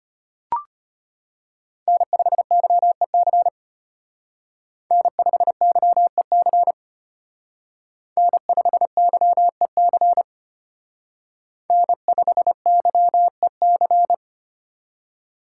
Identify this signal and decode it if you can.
Morse code, heard as a keyed tone, about 29 wpm, 700 Hz. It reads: N5YEC N5YEC N5YEC N5YEC
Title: Speed Racing - USA 1x3 Call Signs 25wpm